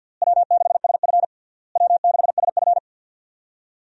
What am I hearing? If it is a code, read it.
Morse code, heard as a keyed tone, about 51 wpm, 700 Hz. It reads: W6SF W6SF
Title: USA Call Signs - 4 Characters - Repeated 3x - 50wpm